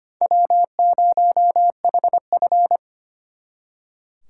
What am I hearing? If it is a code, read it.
Morse code, heard as a keyed tone, about 32 wpm, 700 Hz. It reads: W0HF